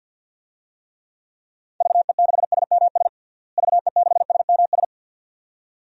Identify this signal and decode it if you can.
Morse code, heard as a keyed tone, about 47 wpm, 700 Hz. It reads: VE6SMS VE6SMS
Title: DXCC-100 Call Signs - 6 Characters - Repeated 2x - 50wpm